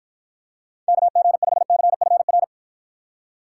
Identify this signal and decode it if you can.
Morse code, heard as a keyed tone, about 53 wpm, 700 Hz. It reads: KG5CFR